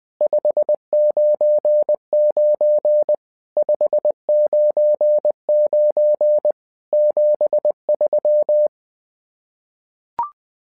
Morse code, heard as a keyed tone, about 20 wpm, 600 Hz. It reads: 599 599 73
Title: SOTA QSO Practice 20wpm